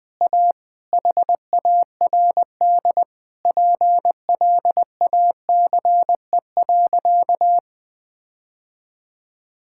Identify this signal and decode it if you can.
Morse code, heard as a keyed tone, about 21 wpm, 700 Hz. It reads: A HARD PLACE.
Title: Sentences with 7-Word Phrases - 20wpm